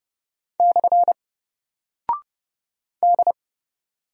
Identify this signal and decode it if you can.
Morse code, heard as a keyed tone, about 30 wpm, 700 Hz. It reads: / D
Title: Single Character (Letter:Number:Punctuation:Prosign) Alt 30wpm